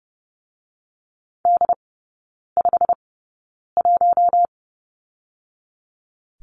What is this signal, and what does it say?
Morse code, heard as a keyed tone, about 30 wpm, spaced as for 9 wpm, 700 Hz. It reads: D51
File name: course / 30:10wpm / 087 Lesson 15e - Number 5 - Three Characters 30wpm